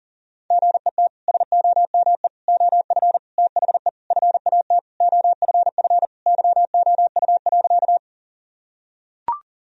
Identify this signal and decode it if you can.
Morse code, heard as a keyed tone, about 40 wpm, 700 Hz. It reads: GET SOME OF THE FAT OFF YOU.
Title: Sentences from Top 1000 Words - Review Part 4 - 40wpm